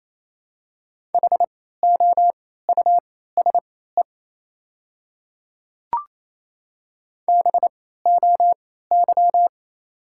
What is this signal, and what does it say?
Morse code, heard as a keyed tone, about 28 wpm, spaced as for 16 wpm, 690 Hz. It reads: HOUSE BOY
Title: 3x Character Spacing - Top 200 Words - 28wpm